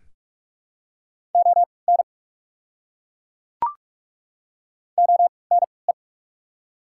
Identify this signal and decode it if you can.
Morse code, heard as a keyed tone, about 41 wpm, spaced as for 25 wpm, 690 Hz. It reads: ON ONE
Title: 3x Character Spacing - Top 200 Words - 45wpm